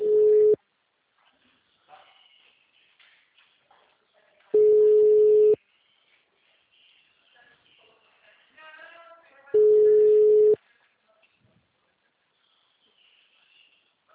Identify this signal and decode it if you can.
Ringback tone (Europe, 425 Hz).